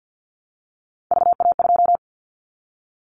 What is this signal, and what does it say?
Morse code, heard as a keyed tone, about 50 wpm, 700 Hz. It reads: VA2
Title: Callsign Prefixes for Top 50 DXCC - 50wpm